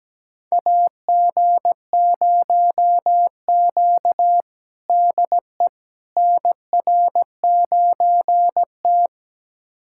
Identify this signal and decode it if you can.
Morse code, heard as a keyed tone, about 17 wpm, 700 Hz. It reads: AG0Q DE NR9T
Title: USA Callsign DE Callsign - 4 Characters - 17wpm